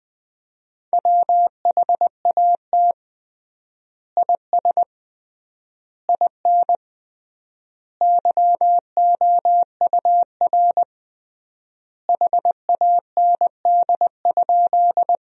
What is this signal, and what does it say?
Morse code, heard as a keyed tone, about 20 wpm, 700 Hz. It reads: WHAT IS IN YOUR HAND?
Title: 3x Word-Spacing - Sentences from Top 200 Words - 20wpm